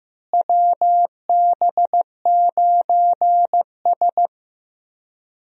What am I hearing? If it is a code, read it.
Morse code, heard as a keyed tone, about 15 wpm, 700 Hz. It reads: WB9S
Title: USA Call Signs - 4 Characters - Repeated 2x - 15wpm